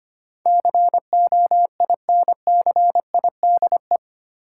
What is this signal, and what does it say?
Morse code, heard as a keyed tone, about 25 wpm, 700 Hz. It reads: COINCIDE
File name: Eight Letter Words - 25wpm